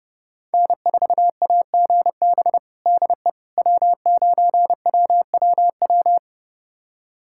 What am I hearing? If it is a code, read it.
Morse code, heard as a keyed tone, about 30 wpm, 700 Hz. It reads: N4AGB DE W9WWW
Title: USA Callsign DE Callsign - 5 Characters - 30wpm